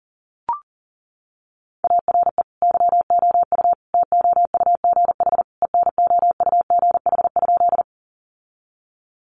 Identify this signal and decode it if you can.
Morse code, heard as a keyed tone, about 40 wpm, 700 Hz. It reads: ARE YOU TOUGH ENOUGH?
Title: Sentences from Top 1200 Words 40wpm